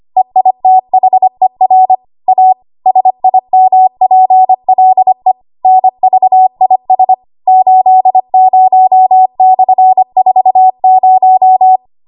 Morse code, heard as a keyed tone, about 25 wpm, 750 Hz. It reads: EITHER A SIMPLE NVIS 80/40
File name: Archive / 25 WPM / 25WPM